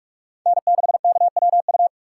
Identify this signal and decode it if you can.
Morse code, heard as a keyed tone, about 45 wpm, 700 Hz. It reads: N6KWU
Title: USA Call Signs - 5 Characters - 3x Character Spacing - 45wpm